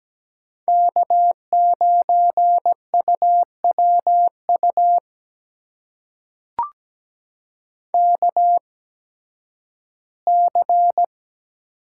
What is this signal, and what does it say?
Morse code, heard as a keyed tone, about 17 wpm, spaced as for 11 wpm, 700 Hz. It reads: K9UWU KC
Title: USA Call Signs - 5 Characters - 8x Character Spacing - 17wpm